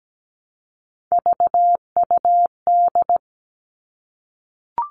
Morse code, heard as a keyed tone, about 17 wpm, 700 Hz. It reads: VUD
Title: Three Challenging Letter-Number Combinations 17wpm